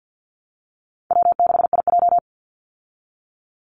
Morse code, heard as a keyed tone, about 50 wpm, 700 Hz. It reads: W6IJ
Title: Call Signs Repeated 2x - 50wpm